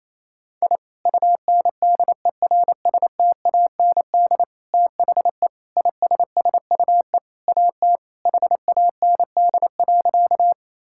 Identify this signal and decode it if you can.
Morse code, heard as a keyed tone, about 28 wpm, 700 Hz. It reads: I UNDERSTAND THE ISSUE AT HAND.
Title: Sentences from Top 1000 Words - Review Part 6 - 28wpm